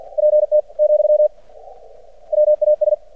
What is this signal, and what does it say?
Morse code, heard as a keyed tone, about 50 wpm, 600 Hz. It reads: PT , WAS
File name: archived / CRSnet